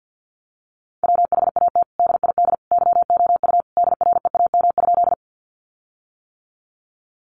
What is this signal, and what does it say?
Morse code, heard as a keyed tone, about 50 wpm, 700 Hz. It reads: WHAT DID YOU DREAM?
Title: Sentences from Top 1500 Words 50wpm